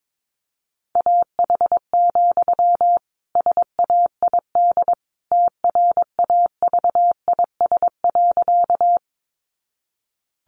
Morse code, heard as a keyed tone, about 22 wpm, 700 Hz. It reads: AH, SAID TRAVIS.
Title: A Sound Of Thunder 22wpm